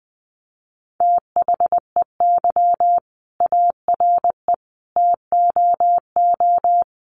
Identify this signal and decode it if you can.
Morse code, heard as a keyed tone, about 20 wpm, 700 Hz. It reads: THEY ARE TOO
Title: Sentences from Top 1400 Words 20wpm